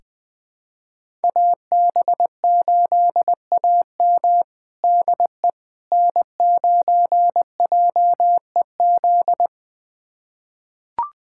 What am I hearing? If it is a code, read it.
Morse code, heard as a keyed tone, about 20 wpm, 700 Hz. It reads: AB8AM DE N9JEZ
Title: USA Callsign DE Callsign - 5 Characters - 20wpm